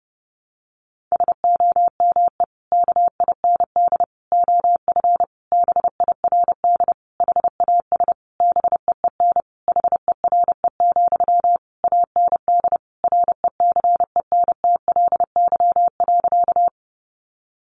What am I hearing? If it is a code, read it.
Morse code, heard as a keyed tone, about 30 wpm, 700 Hz. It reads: SOME KIND OF BIRD HAS BEEN HERE, AND RECENTLY.
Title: Sentences from Top 1200 Words 30wpm